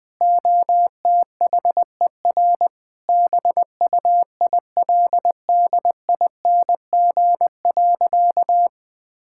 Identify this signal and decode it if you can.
Morse code, heard as a keyed tone, about 20 wpm, 700 Hz. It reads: OTHER BUILDING.
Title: Sentences with 3-Word Phrases - 20wpm